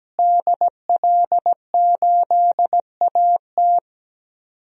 Morse code, heard as a keyed tone, about 17 wpm, 700 Hz. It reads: DL8AT
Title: DXCC-100 Call Signs - 5 Characters - Repeated 3x - 17wpm